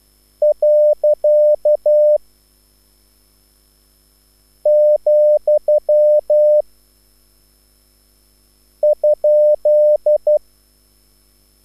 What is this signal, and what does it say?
Morse code, heard as a keyed tone, about 12 wpm, 600 Hz. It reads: . , ?
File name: QSL / CW lesson